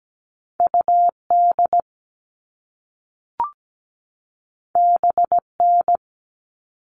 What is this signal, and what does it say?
Morse code, heard as a keyed tone, about 17 wpm, 700 Hz. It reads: UD BN
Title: All Cut Numbers - 2-Digit 17wpm